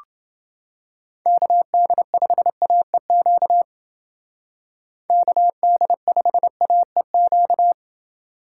Morse code, heard as a keyed tone, about 29 wpm, 700 Hz. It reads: KD5AEQ KD5AEQ